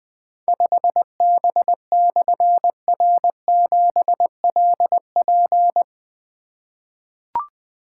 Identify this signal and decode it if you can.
Morse code, heard as a keyed tone, about 20 wpm, 700 Hz. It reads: 5B/R7LP
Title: Difficult Call Signs 20wpm